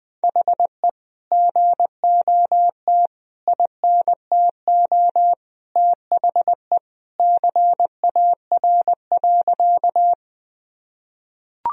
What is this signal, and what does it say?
Morse code, heard as a keyed tone, about 20 wpm, 700 Hz. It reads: HE GOT INTO THE CAR.
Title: Sentences from Top 1000 Words - Review Part 1 - 20wpm